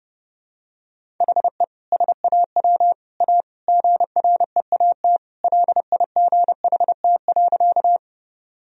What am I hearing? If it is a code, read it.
Morse code, heard as a keyed tone, about 32 wpm, 700 Hz. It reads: HE SAW A GREAT LIGHT.
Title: Sentences from Top 1000 Words - Review Part 1 - 30wpm